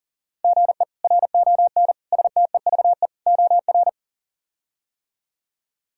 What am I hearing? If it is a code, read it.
Morse code, heard as a keyed tone, about 40 wpm, 700 Hz. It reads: GE RON STEVE OR
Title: K1USN SST QSO Practice - 40wpm